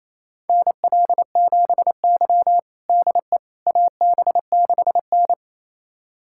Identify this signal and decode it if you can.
Morse code, heard as a keyed tone, about 28 wpm, 700 Hz. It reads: NL7Y DE AB6N
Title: USA Callsign DE Callsign - 4 Characters - 28wpm